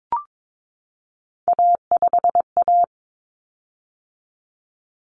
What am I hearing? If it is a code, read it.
Morse code, heard as a keyed tone, about 22 wpm, 700 Hz. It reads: A5A
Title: Common Cut Numbers - 3-Digit 22wpm